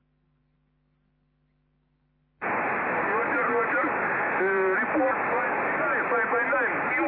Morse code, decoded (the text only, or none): none